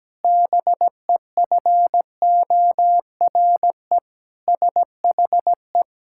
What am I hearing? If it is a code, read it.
Morse code, heard as a keyed tone, about 17 wpm, 700 Hz. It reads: BEFORE SHE
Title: Pangram Challenge v2 - 17wpm